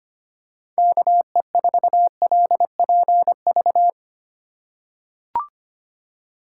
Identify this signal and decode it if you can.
Morse code, heard as a keyed tone, about 25 wpm, 700 Hz. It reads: KE4LPV
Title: USA Call Signs - 3x Character Spacing - 25wpm